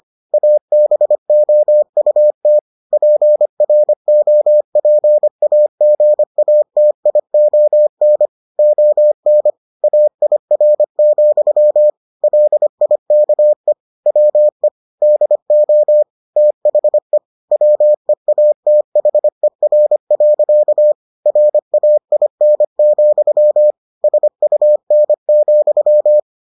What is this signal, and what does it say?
Morse code, heard as a keyed tone, about 25 wpm, 600 Hz. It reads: ABOUT PROPAGATION ON AIR, LIKE WE DO THE WEATHER. RAIN, SUN,